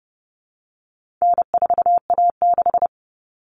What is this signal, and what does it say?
Morse code, heard as a keyed tone, about 30 wpm, 700 Hz. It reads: N4A6
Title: Common Cut Numbers - 4-Digit 30wpm